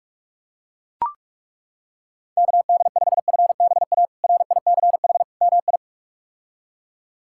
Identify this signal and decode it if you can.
Morse code, heard as a keyed tone, about 41 wpm, 700 Hz. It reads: KD5FBA RICH MI
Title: K1USN SST QSO Practice - 45wpm